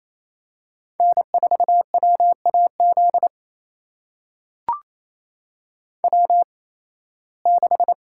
Morse code, heard as a keyed tone, about 28 wpm, spaced as for 18 wpm, 700 Hz. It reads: N4WAZ W6
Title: USA Call Signs - 5 Characters - 8x Character Spacing - 28wpm